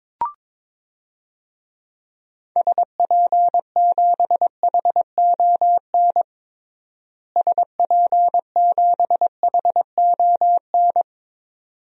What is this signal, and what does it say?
Morse code, heard as a keyed tone, about 22 wpm, 700 Hz. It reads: SP7HON SP7HON
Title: DXCC-100 Call Signs - 6 Characters - Repeated 2x - 22wpm